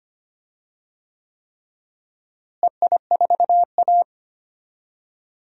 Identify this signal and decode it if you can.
Morse code, heard as a keyed tone, about 25 wpm, 700 Hz. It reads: EI4A
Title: DXCC-100 Call Signs - 4 Characters - Repeated 3x - 25wpm